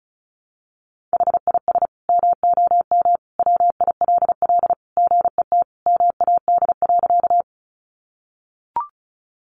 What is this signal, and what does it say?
Morse code, heard as a keyed tone, about 34 wpm, 700 Hz. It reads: HIS MOM WILL GET MAD.
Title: Additional Easy Phrases 35wpm